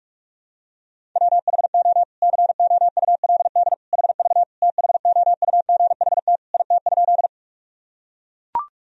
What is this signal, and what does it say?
Morse code, heard as a keyed tone, about 41 wpm, 700 Hz. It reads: WHO COULD HV THOUGHT IT?
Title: Sentences from Top 300 Words and Abbreviations 45wpm